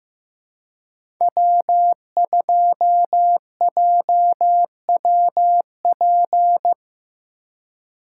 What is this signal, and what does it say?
Morse code, heard as a keyed tone, about 15 wpm, 700 Hz. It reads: W2JWP
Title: USA Call Signs - 4x Character Spacing - 15wpm